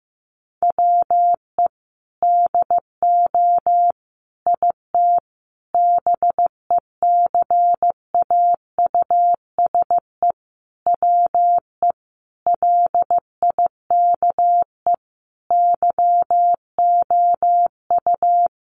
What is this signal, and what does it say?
Morse code, heard as a keyed tone, about 15 wpm, 700 Hz. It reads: WE DO IT BECAUSE WE LIKE YOU
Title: Sentences from Top 1400 Words 15wpm